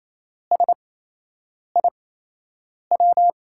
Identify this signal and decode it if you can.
Morse code, heard as a keyed tone, about 28 wpm, spaced as for 8 wpm, 700 Hz. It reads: SIW
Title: USA Call Signs - 8x Character Spacing - 28wpm